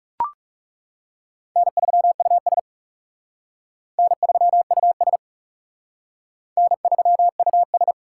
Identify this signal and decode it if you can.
Morse code, heard as a keyed tone, about 38 wpm, 700 Hz. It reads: N3US N3US N3US
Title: Speed Racing - USA 1x2 Call Signs 30wpm